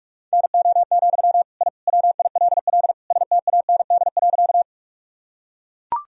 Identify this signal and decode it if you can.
Morse code, heard as a keyed tone, about 43 wpm, 700 Hz. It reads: NO, I WILL STAND.
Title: Sentences from Top 1000 Words - Review Part 1 - 45wpm